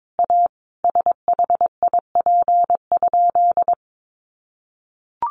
Morse code, heard as a keyed tone, about 22 wpm, 700 Hz. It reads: A SHIP?
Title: Sentences from Top 400 Words 22wpm